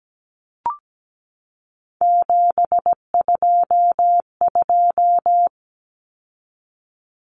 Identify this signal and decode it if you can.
Morse code, heard as a keyed tone, about 17 wpm, 700 Hz. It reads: 722
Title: Numbers - 3-Digit 17wpm